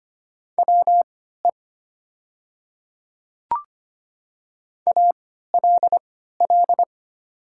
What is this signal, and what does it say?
Morse code, heard as a keyed tone, about 25 wpm, spaced as for 14 wpm, 690 Hz. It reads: WE ALL
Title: 3x Character Spacing - Top 300 Words - 25wpm